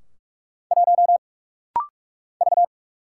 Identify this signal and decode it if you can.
Morse code, heard as a keyed tone, about 46 wpm, 700 Hz. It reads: 1 V
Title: Single Letter-Number - Rapid-Fire 45wpm